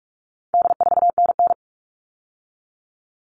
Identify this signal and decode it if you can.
Morse code, heard as a keyed tone, about 44 wpm, 700 Hz. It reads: D4NN
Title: All Cut Numbers - 4-Digit 45wpm